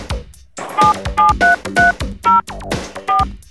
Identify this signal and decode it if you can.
DTMF, heard as telephone touch tones, tones of about 170 ms. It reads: 77330 4